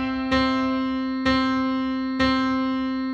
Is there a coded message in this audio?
No.